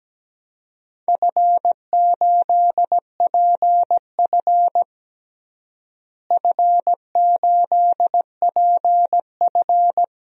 Morse code, heard as a keyed tone, about 17 wpm, 700 Hz. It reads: F8PF F8PF
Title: DXCC-100 Call Signs - 4 Characters - Repeated 2x - 17wpm